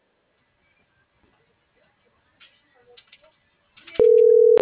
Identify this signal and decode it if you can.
Ringback tone (North America), 440 + 480 Hz.